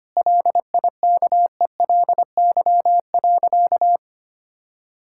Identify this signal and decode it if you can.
Morse code, heard as a keyed tone, about 25 wpm, 700 Hz. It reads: LIKELY.
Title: Sentences from Top 1600 Words 25wpm